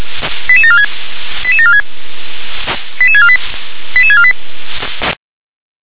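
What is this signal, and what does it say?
Five-tone selective call (ZVEI), heard as quick runs of five tones, tones of about 70 ms. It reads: 88746 88746 88648 88648
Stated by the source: ZVEI five-tone call received on SDR.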